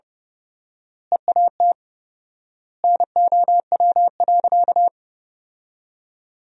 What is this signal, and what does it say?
Morse code, heard as a keyed tone, about 32 wpm, 700 Hz. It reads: EAT NOW.
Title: 4x Word-Spacing - Sentences from Top 300 Words - 30wpm